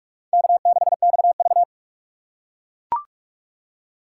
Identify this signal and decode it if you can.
Morse code, heard as a keyed tone, about 45 wpm, 700 Hz. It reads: K6XV
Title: USA Call Signs - 4 Characters - 4x Character Spacing - 45wpm